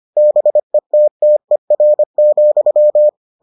Morse code, heard as a keyed tone, about 25 wpm, 600 Hz. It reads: BETTER,